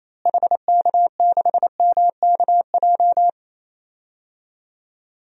Morse code, heard as a keyed tone, about 28 wpm, 700 Hz. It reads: HK6MKJ
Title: DXCC-100 Call Signs - 6 Characters - Repeated 3x - 28wpm